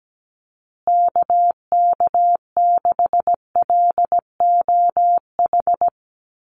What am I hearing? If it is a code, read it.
Morse code, heard as a keyed tone, about 17 wpm, 700 Hz. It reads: KK6LOH
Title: USA Call Signs Repeated 2x 17wpm